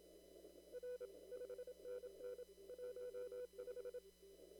Morse code, heard as a keyed tone, about 27 wpm, 500 Hz. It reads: R 5NN 15
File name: CW / kt5j wpx cw-op-k5pi